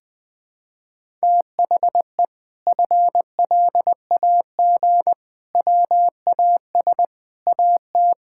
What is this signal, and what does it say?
Morse code, heard as a keyed tone, about 20 wpm, 700 Hz. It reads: THE FLAG WAS AT
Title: Sentences with 5-Word Phrases - Spoken Beginning - 20wpm